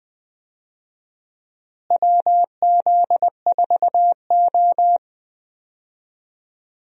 Morse code, heard as a keyed tone, about 20 wpm, 700 Hz. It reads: WZ4O
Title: USA Call Signs - 4 Characters - Repeated 2x - 20wpm